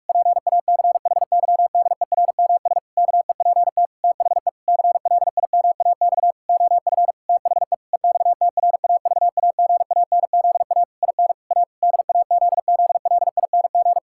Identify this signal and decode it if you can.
Morse code, heard as a keyed tone, about 46 wpm, 700 Hz. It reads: PACHYDERMS KEPT THE CLIMAX OF THE EXTRAVAGANZA IN A DAZZLING
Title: Pangram Challenge v2 - 45wpm